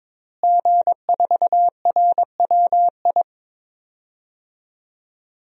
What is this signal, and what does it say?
Morse code, heard as a keyed tone, about 21 wpm, 700 Hz. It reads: G4RWI
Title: DXCC-100 Call Signs - 5 Characters - Repeated 2x - 22wpm